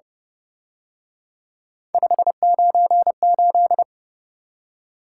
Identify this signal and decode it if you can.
Morse code, heard as a keyed tone, about 29 wpm, 700 Hz. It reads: 598
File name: Speed-Racing - Three Letter-Number Combinations - 25wpm